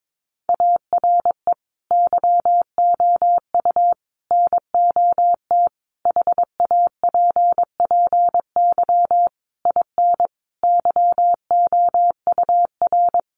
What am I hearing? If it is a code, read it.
Morse code, heard as a keyed tone, about 22 wpm, 700 Hz. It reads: ARE YOU NOT HAPPY IN YOUR
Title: Sentences from Top 1000 Words 22wpm